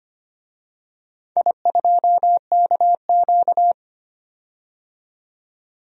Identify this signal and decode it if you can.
Morse code, heard as a keyed tone, about 25 wpm, 690 Hz. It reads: I2KQ
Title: DXCC-100 Call Signs - 4 Characters - 25wpm